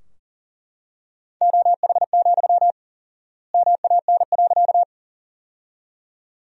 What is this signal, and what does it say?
Morse code, heard as a keyed tone, about 40 wpm, 700 Hz. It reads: OH, MAN.
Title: 4x Word-Spacing - Sentences from Top 500 Words - 40wpm